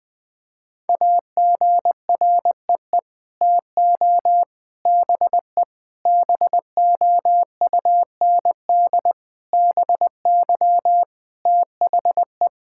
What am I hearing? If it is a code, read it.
Morse code, heard as a keyed tone, about 21 wpm, 700 Hz. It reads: AGREE TO BE BOUND BY THE
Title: Sets of 9 Words - 20wpm